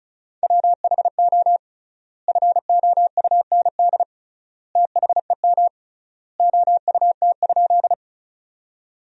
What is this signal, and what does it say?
Morse code, heard as a keyed tone, about 36 wpm, 700 Hz. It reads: WHO FOUND THEM OUT?
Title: 3x Word-Spacing - Sentences from Top 300 Words - 35wpm